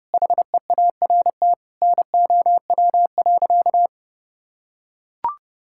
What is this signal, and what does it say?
Morse code, heard as a keyed tone, about 30 wpm, 700 Hz. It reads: HEART NOW.
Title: Sentences from Top 1600 Words 30wpm